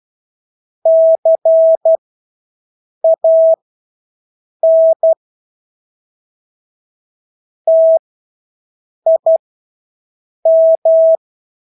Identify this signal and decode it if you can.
Morse code, heard as a keyed tone, about 12 wpm, spaced as for 6 wpm, 650 Hz. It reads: CAN TIM
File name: Engelske ord 12-6 wpm